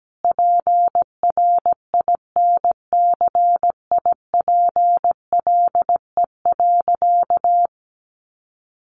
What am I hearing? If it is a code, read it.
Morse code, heard as a keyed tone, about 17 wpm, 700 Hz. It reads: PRINCIPLE.
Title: Sentences from Top 1400 Words 17wpm